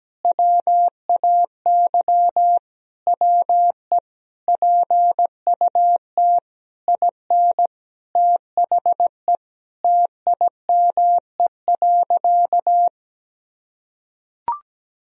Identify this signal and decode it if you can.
Morse code, heard as a keyed tone, about 17 wpm, 700 Hz. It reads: WAY WE PUT IN THE TIME.
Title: Sentences from Top 200 Words and Abbreviations 17wpm